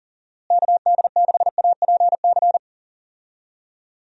Morse code, heard as a keyed tone, about 40 wpm, 700 Hz. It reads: KD6APC
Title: USA Call Signs - 6 Characters - Repeated 2x - 40wpm